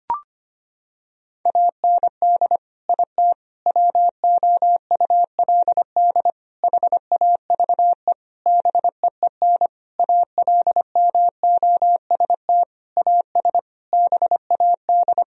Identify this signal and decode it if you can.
Morse code, heard as a keyed tone, about 25 wpm, 700 Hz. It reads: AND IT WOULD HAVE BEEN ALMOST AS BAD
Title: Sentences from Top 1000 Words - Review Part 4 - 25wpm